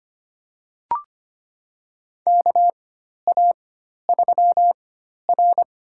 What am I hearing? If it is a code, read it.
Morse code, heard as a keyed tone, about 25 wpm, spaced as for 12 wpm, 700 Hz. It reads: KA3R
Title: USA Call Signs - 4 Characters - 4x Character Spacing - 25wpm